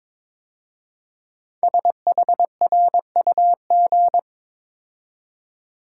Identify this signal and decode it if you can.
Morse code, heard as a keyed tone, about 22 wpm, 700 Hz. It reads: SHRUG
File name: 5 Letter Words - First 3 Letters Given - 22wpm